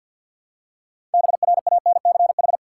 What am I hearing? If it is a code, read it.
Morse code, heard as a keyed tone, about 52 wpm, 700 Hz. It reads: BRANCH
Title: Six Letter Words - 50wpm